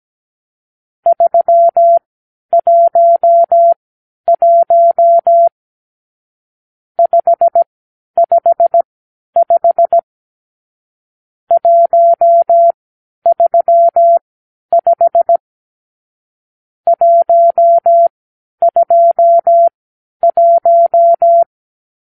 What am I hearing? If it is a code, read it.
Morse code, heard as a keyed tone, about 17 wpm, spaced as for 10 wpm, 680 Hz. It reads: 311 555 135 121
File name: arrl / Review (1-5)